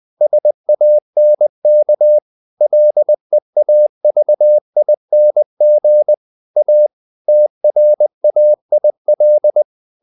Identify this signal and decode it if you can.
Morse code, heard as a keyed tone, about 20 wpm, 600 Hz. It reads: SANK LEAVING A TRAIL
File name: the old man and the sea part 2 20wpm